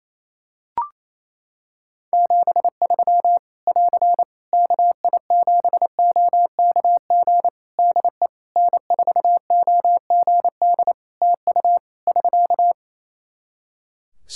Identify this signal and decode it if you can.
Morse code, heard as a keyed tone, about 27 wpm, 700 Hz. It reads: 73 <AR> KI7OKG DE N4OGD TU <SK>
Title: K1USN SST QSO Practice 28wpm